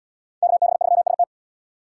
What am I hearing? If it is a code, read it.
Morse code, heard as a keyed tone, about 77 wpm, 700 Hz. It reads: KB3ST